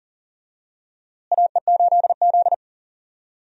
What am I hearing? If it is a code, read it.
Morse code, heard as a keyed tone, about 37 wpm, 700 Hz. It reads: AE8Z
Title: USA Call Signs - 4 Characters - Repeated 3x - 40wpm